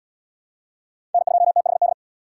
Morse code, heard as a keyed tone, about 74 wpm, 700 Hz. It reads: N2THD